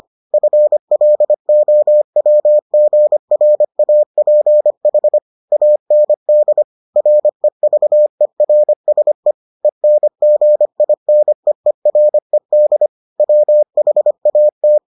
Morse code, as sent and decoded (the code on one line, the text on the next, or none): ..-. .-.. --- .-- --. .-. .- .--. .... / .- -. -.. / .-. ...- . .-. ... / . -. --. .. -. . .-. -.. / .-- .... .- -
FLOWGRAPH AND REVERSE ENGINEERED WHAT